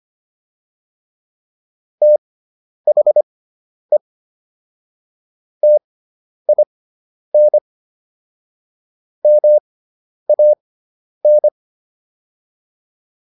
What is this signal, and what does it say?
Morse code, heard as a keyed tone, about 25 wpm, spaced as for 10 wpm, 600 Hz. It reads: THE TIN MAN